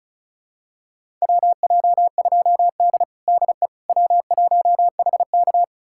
Morse code, heard as a keyed tone, about 36 wpm, 700 Hz. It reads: WJ2D DE W1HK
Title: USA Callsign DE Callsign - 4 Characters - 35wpm